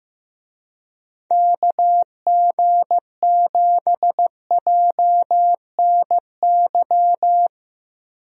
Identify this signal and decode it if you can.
Morse code, heard as a keyed tone, about 15 wpm, 700 Hz. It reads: KG7JNY
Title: USA Call Signs - 6 Characters - Repeated 2x - 15wpm